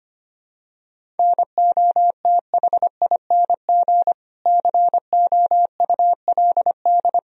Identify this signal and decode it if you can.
Morse code, heard as a keyed tone, about 25 wpm, 700 Hz. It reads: NOTHING COULD
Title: Sentences from Top 500 Words and Abbreviations 25wpm